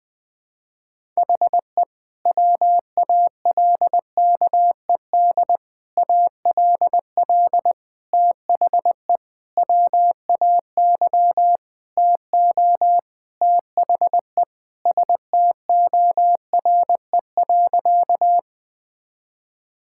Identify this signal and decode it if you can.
Morse code, heard as a keyed tone, about 20 wpm, 700 Hz. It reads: HE WALKED ALL THE WAY TO THE STORE.
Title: Sentences with 5-Word Phrases - 20wpm